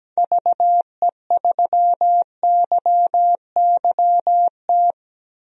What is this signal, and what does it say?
Morse code, heard as a keyed tone, about 17 wpm, 700 Hz. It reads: VE3YYT
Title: DXCC-100 Call Signs - 6 Characters - 17wpm